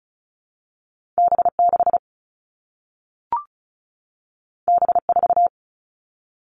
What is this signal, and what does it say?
Morse code, heard as a keyed tone, about 36 wpm, 700 Hz. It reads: B6 B4
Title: All Cut Numbers - 2-Digit 35wpm